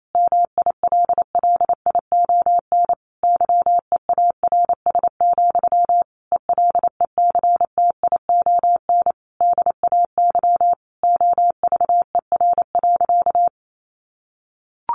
Morse code, heard as a keyed tone, about 28 wpm, 700 Hz. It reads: MILLION YEARS, ELECTION DAY OVER.
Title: A Sound Of Thunder 28wpm